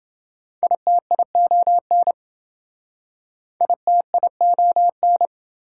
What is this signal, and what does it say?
Morse code, heard as a keyed tone, about 26 wpm, 700 Hz. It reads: ITION ITION